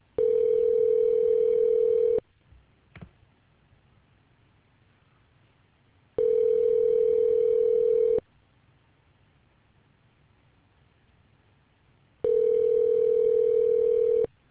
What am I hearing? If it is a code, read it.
Ringback tone (North America), 440 + 480 Hz.